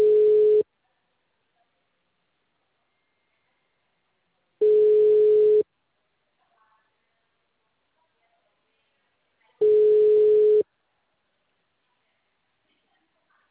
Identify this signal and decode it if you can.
Ringback tone (Europe, 425 Hz).
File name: out- -2023- - - .19287